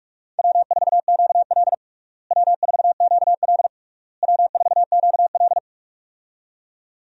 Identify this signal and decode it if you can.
Morse code, heard as a keyed tone, about 46 wpm, 700 Hz. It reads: W4QL W4QL W4QL
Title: USA Call Signs - 4 Characters - Repeated 3x - 45wpm